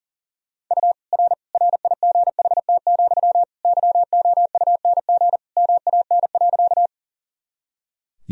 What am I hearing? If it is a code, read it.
Morse code, heard as a keyed tone, about 40 wpm, 700 Hz. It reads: U R RIGHT, YOUNG MAN.
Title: Sentences from Top 400 Words and Abbreviations 40wpm